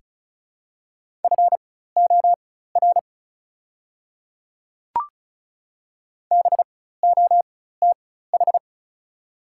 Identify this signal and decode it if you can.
Morse code, heard as a keyed tone, about 36 wpm, spaced as for 16 wpm, 700 Hz. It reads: FOR BOTH
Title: 4x Character Spacing - Top 300 Words - 35wpm